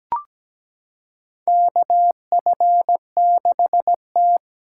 Morse code, heard as a keyed tone, about 17 wpm, 700 Hz. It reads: KF6T
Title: K1USN SST QSO Practice - 17wpm